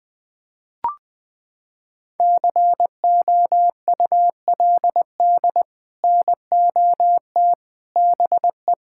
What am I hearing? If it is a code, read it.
Morse code, heard as a keyed tone, about 20 wpm, 700 Hz. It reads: COULD NOT BE
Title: Sets of 6 Words - 20wpm